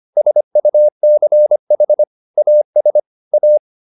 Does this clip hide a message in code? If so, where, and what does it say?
Yes: Morse code from 0.2 to 3.6 s, heard as a keyed tone, about 25 wpm, 600 Hz. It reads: SUCH AS A